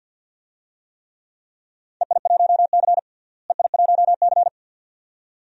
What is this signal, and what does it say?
Morse code, heard as a keyed tone, about 52 wpm, 700 Hz. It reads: EI1C EI1C
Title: DXCC-100 Call Signs - 4 Characters - Repeated 2x - 50wpm